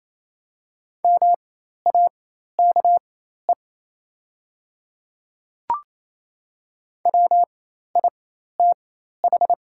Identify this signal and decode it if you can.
Morse code, heard as a keyed tone, about 27 wpm, spaced as for 13 wpm, 700 Hz. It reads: MAKE WITH
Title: 4x Character Spacing - Top 100 Words - 28wpm